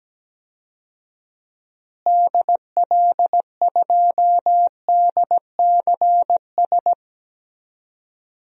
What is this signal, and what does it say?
Morse code, heard as a keyed tone, about 17 wpm, 700 Hz. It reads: DL2DCS
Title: DXCC-100 Call Signs - 6 Characters - Repeated 3x - 17wpm